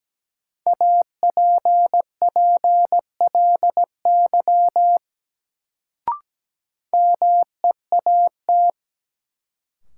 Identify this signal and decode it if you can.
Morse code, heard as a keyed tone, about 17 wpm, 700 Hz. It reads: APPLY MEAT
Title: Sentences from Top 1200 Words - New Word Drill Alt - 17wpm